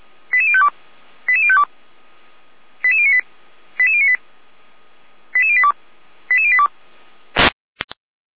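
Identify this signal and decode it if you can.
Five-tone selective call (ZVEI), heard as quick runs of five tones, tones of about 70 ms. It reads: 88052 88052 88098 88098 88082 88082
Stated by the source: Received by SDR; ZVEI five-tone selective call.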